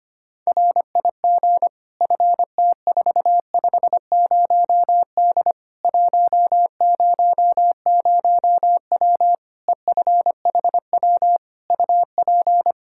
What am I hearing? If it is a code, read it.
Morse code, heard as a keyed tone, about 25 wpm, 700 Hz. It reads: RIG FT450D 100W EFHW UP
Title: K1USN SST QSO Practice 25wpm